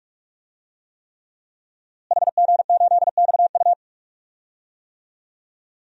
Morse code, heard as a keyed tone, about 46 wpm, 700 Hz. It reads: HG8XU
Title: DXCC-100 Call Signs - 5 Characters - Repeated 3x - 45wpm